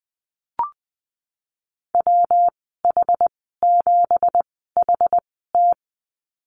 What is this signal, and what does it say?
Morse code, heard as a keyed tone, about 19 wpm, 700 Hz. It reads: W H 7 H T
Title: 207 Lesson 33g - Letter j - Call Signs 20wpm